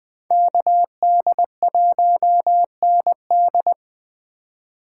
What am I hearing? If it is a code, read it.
Morse code, heard as a keyed tone, about 20 wpm, 700 Hz. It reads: KD1ND